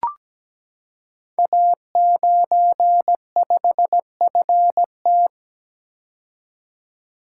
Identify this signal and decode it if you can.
Morse code, heard as a keyed tone, about 17 wpm, 700 Hz. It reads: A95FT
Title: Code Groups - 5 Letter:Numbers - 17wpm